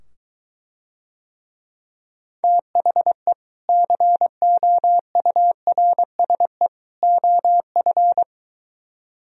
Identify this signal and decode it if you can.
Morse code, heard as a keyed tone, about 24 wpm, 700 Hz. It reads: THE COURSE OF